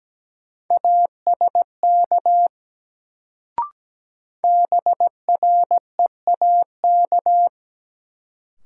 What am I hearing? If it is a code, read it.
Morse code, heard as a keyed tone, about 17 wpm, 700 Hz. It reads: ASK BREAK
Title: Sets of 4 Words - New Word Drill - 17wpm